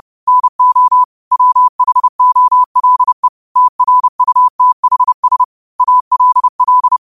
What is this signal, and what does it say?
Morse code, heard as a keyed tone, about 30 wpm, 1000 Hz. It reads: NO WHOLE TRUTHS ALL